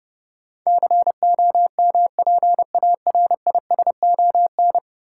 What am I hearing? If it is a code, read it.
Morse code, heard as a keyed tone, about 30 wpm, 700 Hz. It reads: COMPARISON
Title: Sentences from Top 1600 Words - New Word Drill - 30wpm